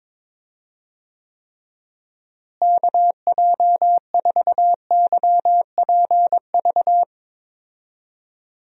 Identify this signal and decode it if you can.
Morse code, heard as a keyed tone, about 22 wpm, 700 Hz. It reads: KJ4YPV
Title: USA Call Signs - 6 Characters - Repeated 3x - 22wpm